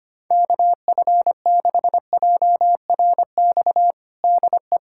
Morse code, heard as a keyed tone, about 25 wpm, 700 Hz. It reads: KF6JRX DE
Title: USA Callsign DE Callsign 25wpm